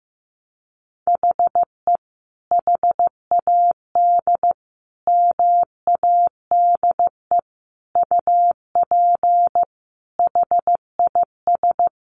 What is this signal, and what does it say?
Morse code, heard as a keyed tone, about 15 wpm, 700 Hz. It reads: HE HAD MADE UP HIS
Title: Sentences from Top 500 Words 15wpm